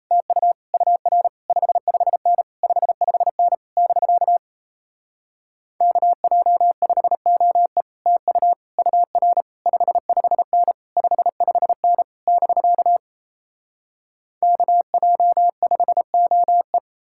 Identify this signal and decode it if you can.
Morse code, heard as a keyed tone, about 38 wpm, 700 Hz. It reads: TU UR 55N 55N <BK> KJ5OE TU UR 55N 55N <BK> KJ5OE